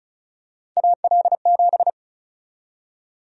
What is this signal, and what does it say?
Morse code, heard as a keyed tone, about 34 wpm, 700 Hz. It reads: AL7
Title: USA Call Sign Prefixes - 35wpm